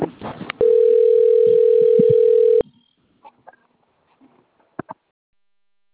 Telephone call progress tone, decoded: ringback tone

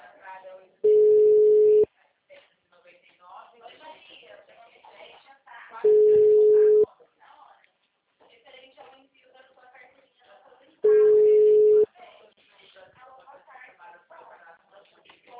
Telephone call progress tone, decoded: ringback tone

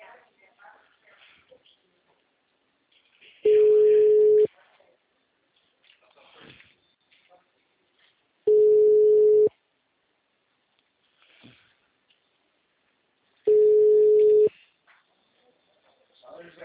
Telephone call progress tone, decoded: ringback tone

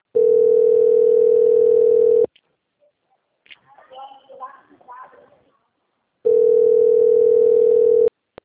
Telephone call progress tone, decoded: ringback tone

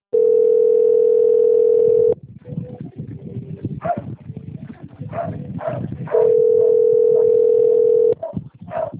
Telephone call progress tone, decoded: ringback tone

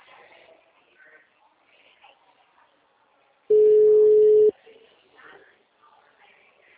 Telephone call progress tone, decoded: ringback tone

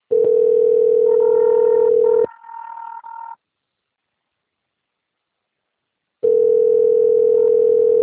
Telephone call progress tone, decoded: ringback tone

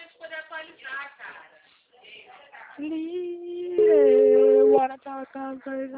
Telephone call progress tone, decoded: ringback tone

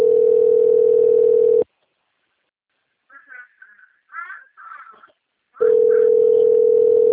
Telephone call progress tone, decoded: ringback tone

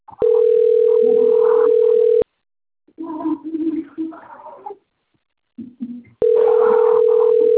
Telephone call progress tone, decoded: ringback tone

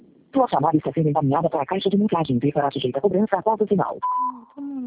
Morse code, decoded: none

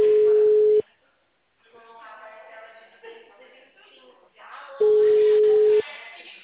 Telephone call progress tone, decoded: ringback tone

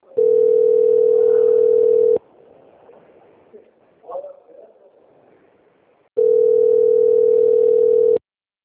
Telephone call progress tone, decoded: ringback tone

